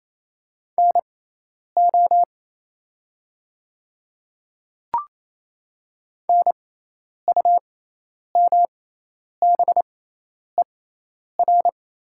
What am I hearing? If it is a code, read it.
Morse code, heard as a keyed tone, about 27 wpm, spaced as for 10 wpm, 700 Hz. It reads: NO NUMBER
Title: 6x Character Spacing - Top 200 Words - 28wpm